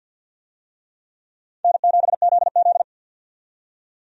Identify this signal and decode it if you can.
Morse code, heard as a keyed tone, about 47 wpm, 700 Hz. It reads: N7ZZ
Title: USA Call Signs - 4 Characters - 50wpm